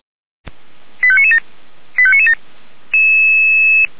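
Five-tone selective call (ZVEI), heard as quick runs of five tones, tones of about 80 ms. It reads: 85997 85997
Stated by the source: Five-tone ZVEI selective call received on SDR, code 89145.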